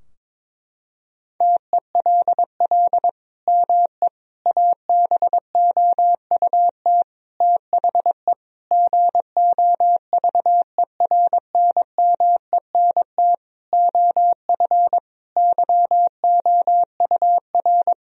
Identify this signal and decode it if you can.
Morse code, heard as a keyed tone, about 22 wpm, 700 Hz. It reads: TELL ME ABOUT THE GOVERNMENT OF YOUR
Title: Sentences from Top 1000 Words - Review Part 3 - 22wpm